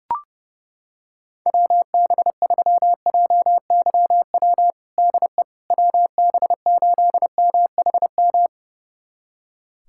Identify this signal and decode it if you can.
Morse code, heard as a keyed tone, about 30 wpm, 700 Hz. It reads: WB3JYW DE WB8MHM
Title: USA Callsign DE Callsign - 6 Characters - 30wpm